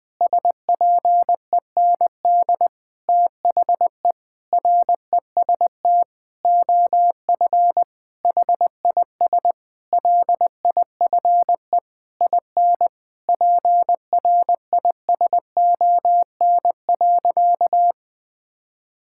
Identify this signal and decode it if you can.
Morse code, heard as a keyed tone, about 21 wpm, 700 Hz. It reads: SPEND THE REST OF HIS LIFE IN PRISON.
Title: Sentences with 8-Word Phrases - 20wpm